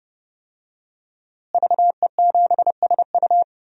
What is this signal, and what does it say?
Morse code, heard as a keyed tone, about 30 wpm, 700 Hz. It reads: VE7SU
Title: DXCC-100 Call Signs - 5 Characters - Repeated 2x - 30wpm